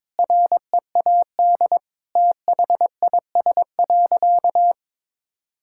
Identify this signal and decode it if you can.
Morse code, heard as a keyed tone, about 22 wpm, 700 Hz. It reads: READ THIS.
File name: practice / 22wpm / Sentences with 8-Word Phrases - 22wpm